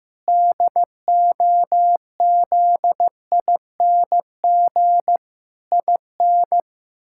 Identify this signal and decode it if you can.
Morse code, heard as a keyed tone, about 15 wpm, 700 Hz. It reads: DOZING IN
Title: Pangram Challenge v2 - 15wpm